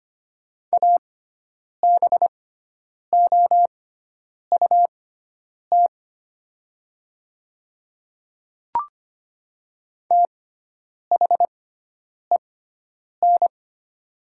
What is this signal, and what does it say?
Morse code, heard as a keyed tone, about 25 wpm, spaced as for 9 wpm, 700 Hz. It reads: ABOUT THEN
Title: 6x Character Spacing - Top 100 Words - 25wpm